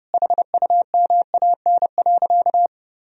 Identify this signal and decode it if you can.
Morse code, heard as a keyed tone, about 32 wpm, 700 Hz. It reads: HUMAN.
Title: Sentences from Top 1000 Words - Review Part 3 - 30wpm